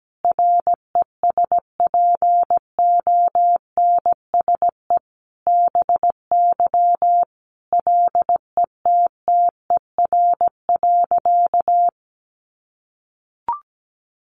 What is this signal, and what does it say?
Morse code, heard as a keyed tone, about 17 wpm, 700 Hz. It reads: RESPONSE BY LETTER.
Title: Sentences from Top 1400 Words 17wpm